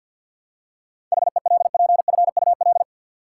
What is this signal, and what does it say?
Morse code, heard as a keyed tone, about 47 wpm, 700 Hz. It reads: HELPFUL
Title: Seven Letter Words - 50wpm